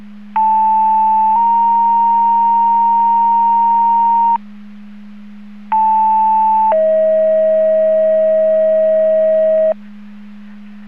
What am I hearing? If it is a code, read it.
Two-tone page, heard as a one-second tone, then a three-second tone, about 880 then 930 Hz, about 880 then 640 Hz. It is heard from 0.4 to 9.7 s.